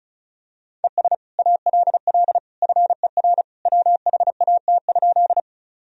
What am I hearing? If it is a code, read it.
Morse code, heard as a keyed tone, about 36 wpm, 700 Hz. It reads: ES ALL FER WHAT?
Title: Sentences from Top 100 Words and Abbreviations 35wpm